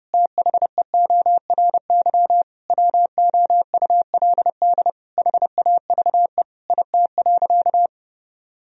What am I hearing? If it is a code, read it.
Morse code, heard as a keyed tone, about 30 wpm, 700 Hz. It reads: THEORY WOULD HAVE IT.
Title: Sentences from Top 1000 Words - Review Part 3 - 30wpm